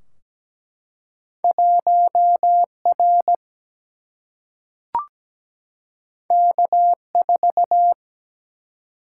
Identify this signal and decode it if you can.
Morse code, heard as a keyed tone, about 17 wpm, 700 Hz. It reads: 1R K4
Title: Single-Letter Single-Number 17wpm